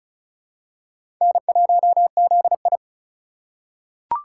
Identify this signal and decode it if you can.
Morse code, heard as a keyed tone, about 36 wpm, 700 Hz. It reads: N1ZI
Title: USA Call Signs - 4 Characters - 2x Character Spacing - 35wpm